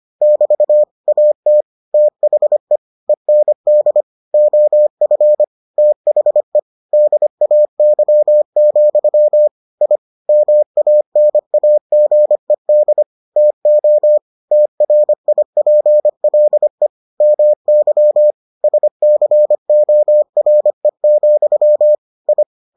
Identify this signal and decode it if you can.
Morse code, heard as a keyed tone, about 25 wpm, 600 Hz. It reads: <BT> AT THE END OF THE DAY, I MANAGED TO TRIPLE MY SCORE, I